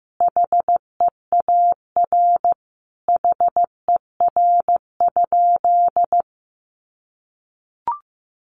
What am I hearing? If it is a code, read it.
Morse code, heard as a keyed tone, about 15 wpm, 700 Hz. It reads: HEAR HER?
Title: Sentences from Top 1100 Words 15wpm